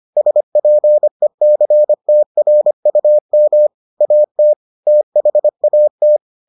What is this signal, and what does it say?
Morse code, heard as a keyed tone, about 25 wpm, 600 Hz. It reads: SPECTRUM AT THAT